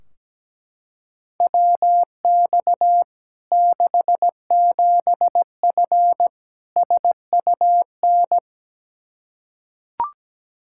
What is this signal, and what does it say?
Morse code, heard as a keyed tone, about 17 wpm, 700 Hz. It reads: WX 67F SUN
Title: K1USN SST QSO Practice 17wpm